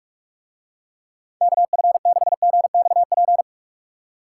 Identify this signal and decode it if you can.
Morse code, heard as a keyed tone, about 44 wpm, 700 Hz. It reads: KF6GXP